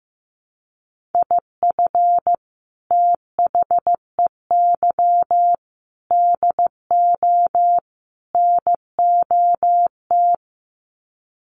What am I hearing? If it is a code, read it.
Morse code, heard as a keyed tone, about 15 wpm, 700 Hz. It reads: IF THEY DO NOT
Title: Sets of 4 Words 15wpm